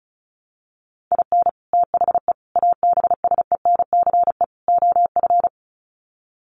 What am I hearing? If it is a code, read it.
Morse code, heard as a keyed tone, about 34 wpm, 700 Hz. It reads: IN THE ABSENCE OF
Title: Sets of 4 Words 35wpm